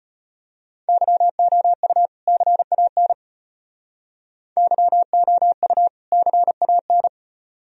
Morse code, heard as a keyed tone, about 33 wpm, 700 Hz. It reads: YOU CAN YOU CAN